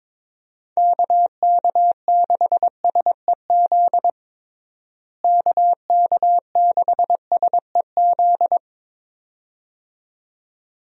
Morse code, heard as a keyed tone, about 22 wpm, 700 Hz. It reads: KK6SEZ KK6SEZ
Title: USA Call Signs - 6 Characters - Repeated 2x - 22wpm